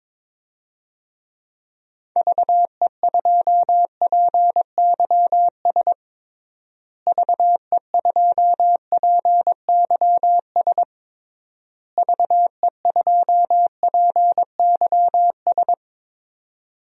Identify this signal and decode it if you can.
Morse code, heard as a keyed tone, about 22 wpm, 700 Hz. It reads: VE2PYS VE2PYS VE2PYS
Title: DXCC-100 Call Signs - 6 Characters - Repeated 3x - 22wpm